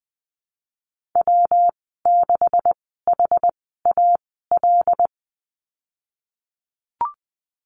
Morse code, heard as a keyed tone, about 20 wpm, spaced as for 14 wpm, 700 Hz. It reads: W6HAL
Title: 128 Lesson 21g - Number 6 - Call Signs 20wpm